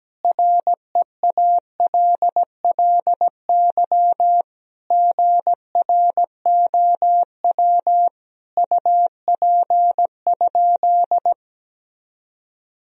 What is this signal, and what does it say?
Morse code, heard as a keyed tone, about 17 wpm, 700 Hz. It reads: REALLY GROW UP?
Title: Sentences from Top 1000 Words - Review Part 1 - 17wpm